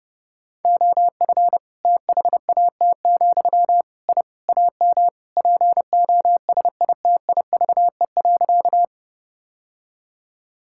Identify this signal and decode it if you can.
Morse code, heard as a keyed tone, about 32 wpm, 700 Hz. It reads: OF THAT, I AM POSITIVE.
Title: Sentences from Top 1000 Words - Review Part 5 - 30wpm